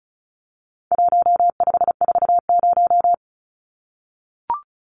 Morse code, heard as a keyed tone, about 34 wpm, 700 Hz. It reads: 1540
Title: Numbers - 4-Digit 35wpm